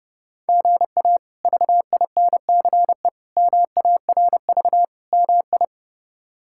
Morse code, heard as a keyed tone, about 30 wpm, 700 Hz. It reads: GA VINCE MARV MI
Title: K1USN SST QSO Practice - 30wpm